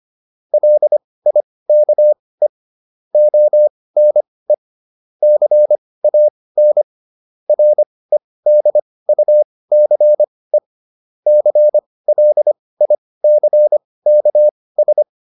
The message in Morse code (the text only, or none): LIKE ONE CAN REDUCE CLICKS